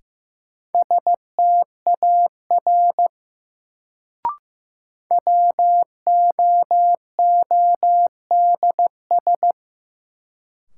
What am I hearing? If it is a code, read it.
Morse code, heard as a keyed tone, about 15 wpm, 700 Hz. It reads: STAR WOODS
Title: Sentences from Top 1000 Words - Review Part 2 - New Word Drill - 15wpm